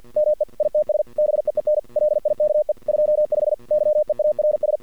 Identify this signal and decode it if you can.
Morse code, heard as a keyed tone, about 50 wpm, 610 Hz. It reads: ME ITS BEEN LIKE 25 YEARS
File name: archived / CRSnet